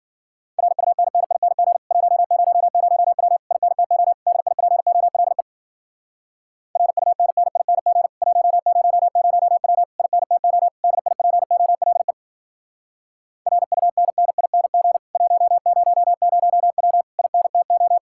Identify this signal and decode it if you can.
Morse code, heard as a keyed tone, about 52 wpm, 700 Hz. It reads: RUNNING 100W INTO DIPOLE RUNNING 100W INTO DIPOLE RUNNING 100W INTO